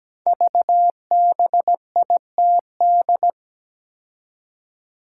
Morse code, heard as a keyed tone, about 17 wpm, 700 Hz. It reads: VBITD
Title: Code Groups - 5 Letters - 17wpm